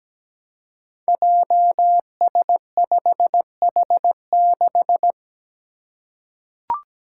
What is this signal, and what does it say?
Morse code, heard as a keyed tone, about 17 wpm, 700 Hz. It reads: JS5H6
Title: Five Challenging Letter-Number Combinations 17wpm